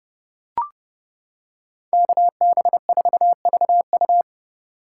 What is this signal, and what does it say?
Morse code, heard as a keyed tone, about 30 wpm, 700 Hz. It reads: KB4VU
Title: K1USN SST QSO Practice - 30wpm